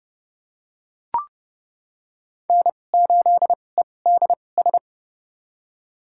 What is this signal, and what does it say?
Morse code, heard as a keyed tone, about 30 wpm, spaced as for 22 wpm, 690 Hz. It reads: N8EDS
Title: USA Call Signs - 5 Characters - 2x Character Spacing - 30wpm